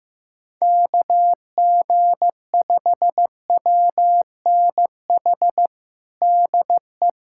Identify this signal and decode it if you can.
Morse code, heard as a keyed tone, about 15 wpm, 700 Hz. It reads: KG5WNH DE
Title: USA Callsign DE Callsign 15wpm